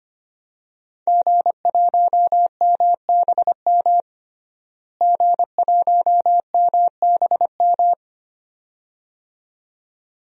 Morse code, heard as a keyed tone, about 25 wpm, 700 Hz. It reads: G1MBM G1MBM
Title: DXCC-100 Call Signs - 5 Characters - Repeated 2x - 25wpm